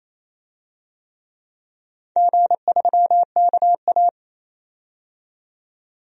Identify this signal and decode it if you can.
Morse code, heard as a keyed tone, about 28 wpm, 700 Hz. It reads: G3KA
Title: DXCC-100 Call Signs - 4 Characters - 28wpm